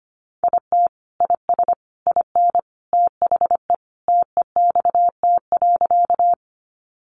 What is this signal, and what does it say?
Morse code, heard as a keyed tone, about 25 wpm, 700 Hz. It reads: IT IS IN THE TEXT.
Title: Sentences from Top 1200 Words 25wpm